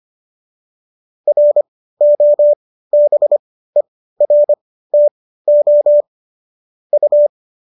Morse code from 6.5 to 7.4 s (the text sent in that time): U